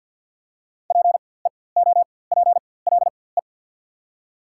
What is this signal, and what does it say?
Morse code, heard as a keyed tone, about 48 wpm, spaced as for 23 wpm, 700 Hz. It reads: PEOPLE
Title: 4x Character Spacing - Top 100 Words - 50wpm